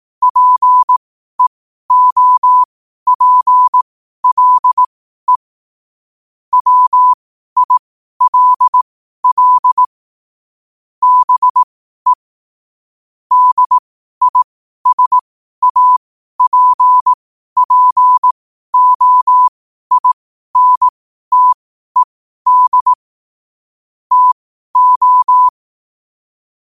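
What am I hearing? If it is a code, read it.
Morse code, heard as a keyed tone, about 18 wpm, spaced as for 12 wpm, 1000 Hz. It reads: PEOPLE WILL BE DISAPPOINTED TO
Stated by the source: Morse code at 12 WPM.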